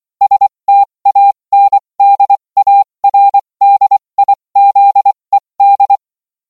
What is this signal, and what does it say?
Morse code, heard as a keyed tone, about 24 wpm, spaced as for 20 wpm, 800 Hz. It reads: STANDARDIZED